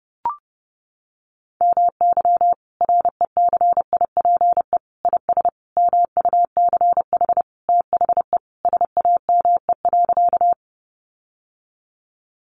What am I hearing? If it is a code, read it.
Morse code, heard as a keyed tone, about 30 wpm, 700 Hz. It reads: MY RECIPE IS MUCH THE SAME.
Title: Sentences from Top 1200 Words 30wpm